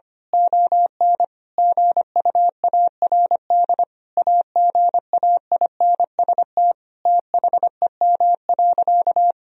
Morse code, heard as a keyed tone, about 25 wpm, 700 Hz. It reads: ON GUARD AGAINST THEM.
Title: Sentences from Top 1000 Words - Review Part 4 - 25wpm